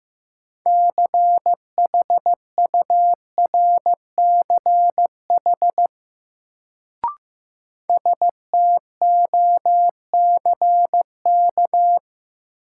Morse code, heard as a keyed tone, about 15 wpm, 700 Hz. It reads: CHURCH STOCK
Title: Sets of 5 Words - New Word Drill - 15wpm